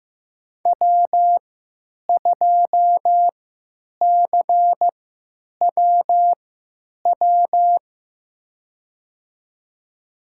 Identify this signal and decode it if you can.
Morse code, heard as a keyed tone, about 15 wpm, spaced as for 9 wpm, 700 Hz. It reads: W2CWW
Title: USA Call Signs - 5 Characters - 3x Character Spacing - 15wpm